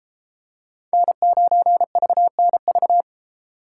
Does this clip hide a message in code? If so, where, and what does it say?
Yes: Morse code from 0.9 to 3.0 s, heard as a keyed tone, about 32 wpm, 700 Hz. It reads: N9VNV